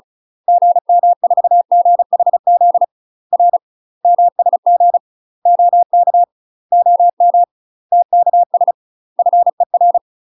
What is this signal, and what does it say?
Morse code, heard as a keyed tone, about 35 wpm, 700 Hz. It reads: GM4GHZ R MSG OK OM TKS FER
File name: UK MASTER QSOs TRAINING / 35WPM / UK MASTER QSOs 0024